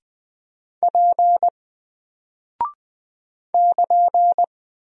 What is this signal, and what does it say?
Morse code, heard as a keyed tone, about 20 wpm, 700 Hz. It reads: P <KN>
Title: Single Character (Letter:Number:Punctuation:Prosign) Alt 20wpm